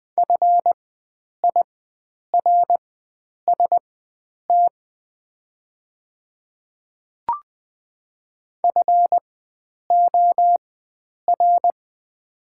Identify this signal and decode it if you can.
Morse code, heard as a keyed tone, about 20 wpm, spaced as for 9 wpm, 700 Hz. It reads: FIRST FOR
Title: 4x Character Spacing - Top 100 Words - 20wpm